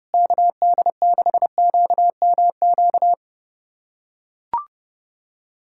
Morse code, heard as a keyed tone, about 28 wpm, 700 Hz. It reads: KD6QMQ
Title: USA Callsign DE Callsign - 6 Characters - 30wpm